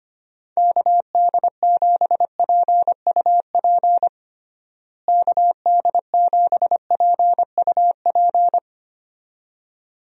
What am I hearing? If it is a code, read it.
Morse code, heard as a keyed tone, about 25 wpm, 700 Hz. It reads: KD7PUP KD7PUP
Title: USA Call Signs - 6 Characters - Repeated 2x - 25wpm